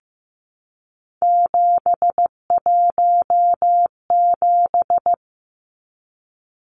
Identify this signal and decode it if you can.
Morse code, heard as a keyed tone, about 15 wpm, 700 Hz. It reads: 717
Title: Numbers - 3-Digit 15wpm